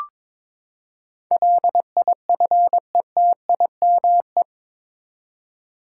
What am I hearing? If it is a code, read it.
Morse code, heard as a keyed tone, about 22 wpm, 700 Hz. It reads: LIFETIME
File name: practice / 22wpm / Eight Letter Words - 22wpm